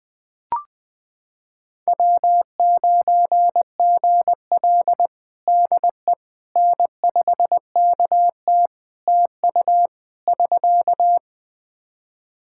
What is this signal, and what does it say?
Morse code, heard as a keyed tone, about 21 wpm, 700 Hz. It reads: W9GL DE N5KT TU <SK>
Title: Practice QSO - Exchange 4 - Ending - 20wpm